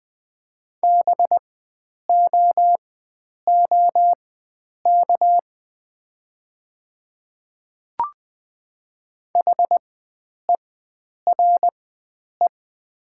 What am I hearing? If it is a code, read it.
Morse code, heard as a keyed tone, about 20 wpm, spaced as for 9 wpm, 700 Hz. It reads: BOOK HERE
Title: 4x Character Spacing - Top 300 Words - 20wpm